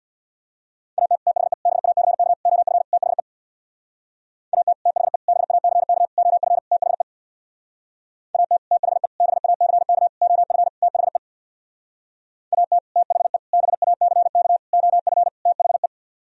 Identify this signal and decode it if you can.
Morse code, heard as a keyed tone, about 64 wpm, 700 Hz. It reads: AT THE BACK OF THE AT THE BACK OF THE AT THE BACK OF THE AT THE BACK OF THE